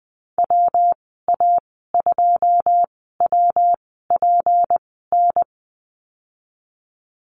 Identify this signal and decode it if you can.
Morse code, heard as a keyed tone, about 20 wpm, spaced as for 15 wpm, 700 Hz. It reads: WA2WPN
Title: 161 Lesson 26g - Letter g - Call Signs 20wpm